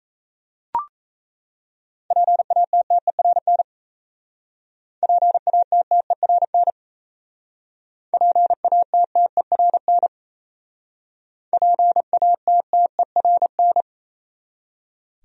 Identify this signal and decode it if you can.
Morse code, heard as a keyed tone, about 35 wpm, 700 Hz. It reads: PATTERN PATTERN PATTERN PATTERN
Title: Speed-Racing Top 500 Words 28wpm